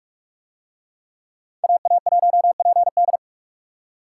Morse code, heard as a keyed tone, about 40 wpm, 700 Hz. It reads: AA1PD